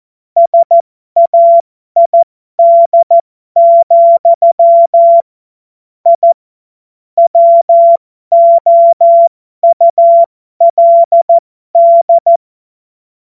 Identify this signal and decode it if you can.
Morse code, heard as a keyed tone, about 14 wpm, spaced as for 12 wpm, 680 Hz. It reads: SAID, I WOULD